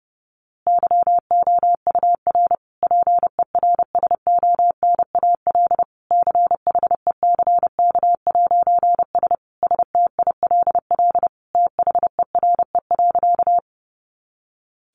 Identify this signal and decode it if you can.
Morse code, heard as a keyed tone, about 30 wpm, 700 Hz. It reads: YOUR PERSONAL CHECK'S STILL THERE.
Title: A Sound Of Thunder 30wpm